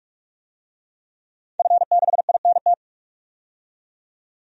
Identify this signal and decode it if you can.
Morse code, heard as a keyed tone, about 46 wpm, 700 Hz. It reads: F6INT
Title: DXCC-100 Call Signs - 5 Characters - 45wpm